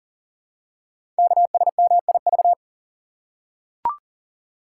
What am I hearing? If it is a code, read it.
Morse code, heard as a keyed tone, about 42 wpm, 700 Hz. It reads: KSMIV
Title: Code Groups - 5 Letters - 40wpm